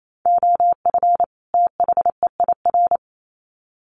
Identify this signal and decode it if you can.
Morse code, heard as a keyed tone, about 28 wpm, 700 Hz. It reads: OF THEIR
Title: Sets of 2 Words 28wpm